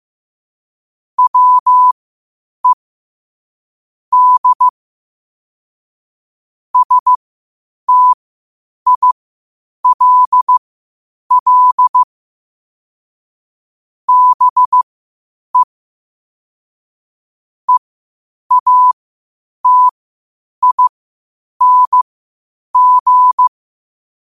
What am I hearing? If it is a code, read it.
Morse code, heard as a keyed tone, about 15 wpm, spaced as for 8 wpm, 1000 Hz. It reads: WE D STILL BE EATING
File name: qotdfiles / 8wpm